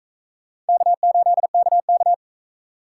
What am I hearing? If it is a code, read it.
Morse code, heard as a keyed tone, about 44 wpm, 700 Hz. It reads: K8KK